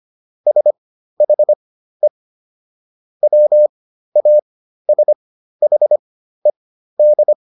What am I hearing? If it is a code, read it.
Morse code, heard as a keyed tone, about 25 wpm, spaced as for 13 wpm, 600 Hz. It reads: SHE WASHED